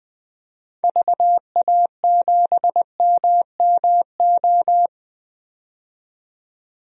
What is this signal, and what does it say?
Morse code, heard as a keyed tone, about 20 wpm, 700 Hz. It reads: VA7MMO
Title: DXCC-100 Call Signs - 6 Characters - 20wpm